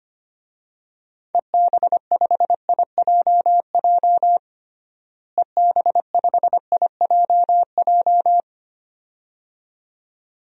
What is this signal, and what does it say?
Morse code, heard as a keyed tone, about 25 wpm, 700 Hz. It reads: EB5IJJ EB5IJJ
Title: DXCC-100 Call Signs - 6 Characters - Repeated 2x - 25wpm